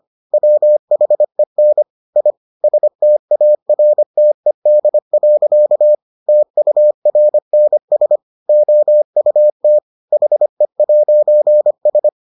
Morse code, heard as a keyed tone, about 25 wpm, 600 Hz. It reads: WHEN I STARTED. TURNS OUT HE'S